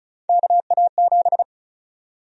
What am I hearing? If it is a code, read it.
Morse code, heard as a keyed tone, about 34 wpm, 700 Hz. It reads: KA7
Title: USA Call Sign Prefixes - 35wpm